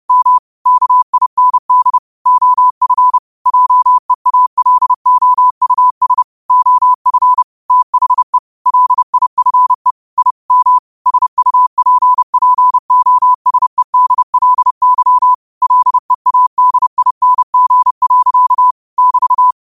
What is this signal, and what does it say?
Morse code, heard as a keyed tone, about 30 wpm, 1000 Hz. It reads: M KIND OF JEALOUS OF THE LIFE I M SUPPOSEDLY LEADING. <BT>